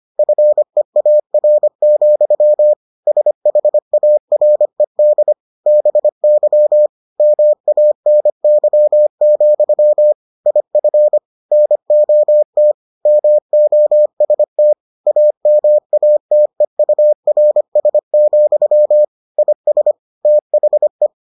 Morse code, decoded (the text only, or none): FEAR, SHARED BY MANY, IF NOT MOST AMATEURS, IS THE